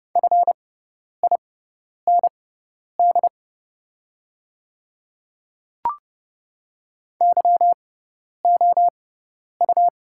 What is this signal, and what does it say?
Morse code, heard as a keyed tone, about 30 wpm, spaced as for 10 wpm, 700 Hz. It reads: FIND YOU
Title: 6x Character Spacing - Top 100 Words - 30wpm